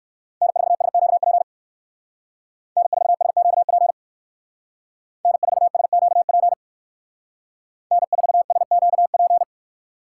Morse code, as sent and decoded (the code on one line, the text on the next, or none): -. ....- ... --.- .--. / -. ....- ... --.- .--. / -. ....- ... --.- .--. / -. ....- ... --.- .--.
N4SQP N4SQP N4SQP N4SQP